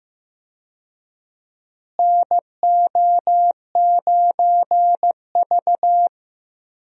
Morse code, heard as a keyed tone, about 15 wpm, 700 Hz. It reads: NO9V
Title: USA Call Signs - 4 Characters - Repeated 2x - 15wpm